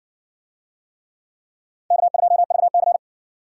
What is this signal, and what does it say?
Morse code, heard as a keyed tone, about 53 wpm, 700 Hz. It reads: K2VC